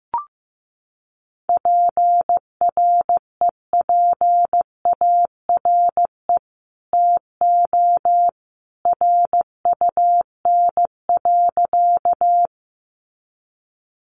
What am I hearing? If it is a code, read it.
Morse code, heard as a keyed tone, about 15 wpm, 700 Hz. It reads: PREPARE TO RUN.
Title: Sentences from Top 900 Words 15wpm